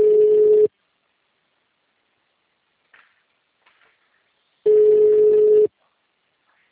Ringback tone (Europe, 425 Hz).